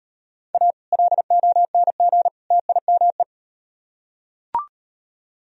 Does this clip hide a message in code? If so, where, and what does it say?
Yes: Morse code from 0.5 to 3.2 s, heard as a keyed tone, about 38 wpm, 700 Hz. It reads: A LONG TIME